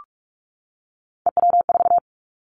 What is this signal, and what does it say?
Morse code, heard as a keyed tone, about 46 wpm, 700 Hz. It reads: EW4
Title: Callsign Prefixes for Top 50 DXCC - 45wpm